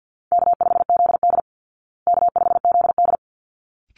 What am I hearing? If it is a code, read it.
Morse code, heard as a keyed tone, about 50 wpm, 700 Hz. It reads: K5ZD K5ZD
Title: Easy Call Signs Repeated 2x - 50wpm